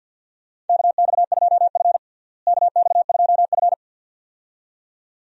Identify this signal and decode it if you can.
Morse code, heard as a keyed tone, about 52 wpm, 700 Hz. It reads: KX2F KX2F
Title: USA Call Signs - 4 Characters - Repeated 2x - 50wpm